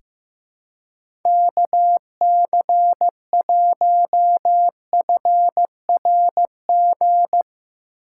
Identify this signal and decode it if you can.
Morse code, heard as a keyed tone, about 16 wpm, 700 Hz. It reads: KC1FRG
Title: USA Call Signs - 6 Characters - 4x Character Spacing - 15wpm